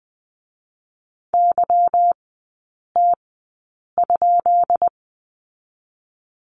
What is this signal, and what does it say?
Morse code, heard as a keyed tone, about 20 wpm, spaced as for 8 wpm, 700 Hz. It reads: YT?
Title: 152 Lesson 25e - Letter p - Three Characters 20wpm